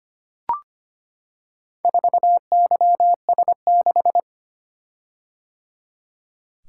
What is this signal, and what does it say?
Morse code, heard as a keyed tone, about 25 wpm, 700 Hz. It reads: 4YS6
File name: practice / 25wpm / Four Letter-Number Combinations - 25wpm